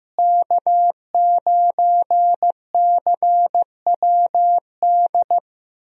Morse code, heard as a keyed tone, about 15 wpm, 700 Hz. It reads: K9CWD
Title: USA Call Signs - 5 Characters - Repeated 2x - 15wpm